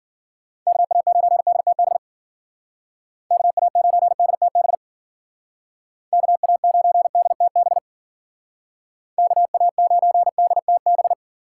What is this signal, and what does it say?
Morse code, heard as a keyed tone, about 51 wpm, 700 Hz. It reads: KA9DTB KA9DTB KA9DTB KA9DTB